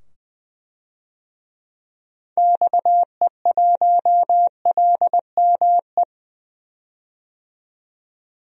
Morse code, heard as a keyed tone, about 20 wpm, 700 Hz. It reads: XE1LME
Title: DXCC-100 Call Signs - 6 Characters - 20wpm